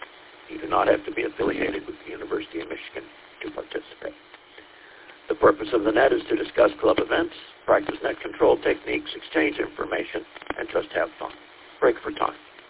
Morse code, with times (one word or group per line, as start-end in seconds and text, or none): none